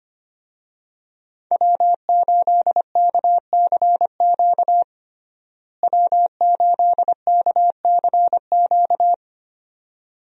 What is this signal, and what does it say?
Morse code, heard as a keyed tone, about 25 wpm, 700 Hz. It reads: W8KCQ W8KCQ
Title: USA Call Signs - 5 Characters - Repeated 2x - 25wpm